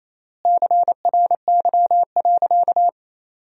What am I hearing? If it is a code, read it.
Morse code, heard as a keyed tone, about 28 wpm, 700 Hz. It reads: CRY.
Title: 3x Word-Spacing - Sentences from Top 500 Words - 28wpm